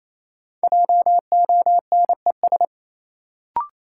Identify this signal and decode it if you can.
Morse code, heard as a keyed tone, about 28 wpm, 700 Hz. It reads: JONES
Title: Sets of 5 Words - New Word Drill - 28wpm